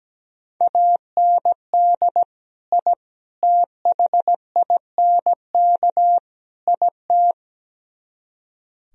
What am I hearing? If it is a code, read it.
Morse code, heard as a keyed tone, about 17 wpm, 700 Hz. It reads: AND I THINK IT
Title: Speed-Racing - Sets of 4 Words 17wpm